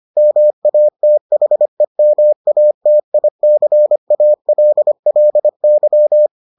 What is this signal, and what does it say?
Morse code, heard as a keyed tone, about 25 wpm, 600 Hz. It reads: MATHEMATICALLY